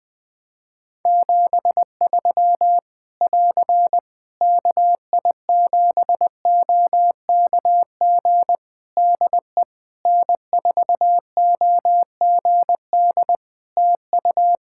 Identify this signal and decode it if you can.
Morse code, heard as a keyed tone, about 20 wpm, 700 Hz. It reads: 73 <AR> KI7OKG DE N4OGD TU
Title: K1USN SST QSO Practice 20wpm